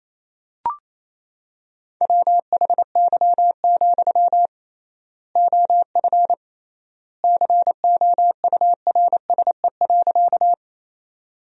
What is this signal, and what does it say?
Morse code, heard as a keyed tone, about 28 wpm, 700 Hz. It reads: WHY, OF COURSE.
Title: 3x Word-Spacing - Sentences from Top 500 Words - 28wpm